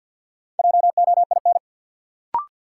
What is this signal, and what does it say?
Morse code, heard as a keyed tone, about 52 wpm, 700 Hz. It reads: JOIN
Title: Sentences from Top 1400 Words - New Word Drill Alt - 50wpm